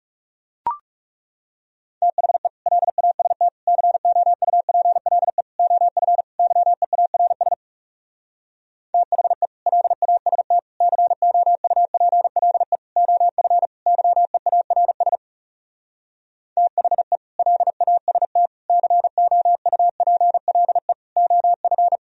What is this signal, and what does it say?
Morse code, heard as a keyed tone, about 40 wpm, 700 Hz. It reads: THE LAST COUPLE OF YEARS THE LAST COUPLE OF YEARS THE LAST COUPLE OF